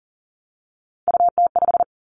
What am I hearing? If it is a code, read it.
Morse code, heard as a keyed tone, about 40 wpm, 700 Hz. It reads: UT5
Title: Callsign Prefixes for Top 25 DXCC - Repeated 3x - 40wpm